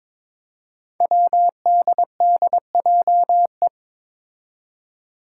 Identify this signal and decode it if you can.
Morse code, heard as a keyed tone, about 22 wpm, 700 Hz. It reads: WDDJE
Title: Five Challenging Letter-Number Combinations 22wpm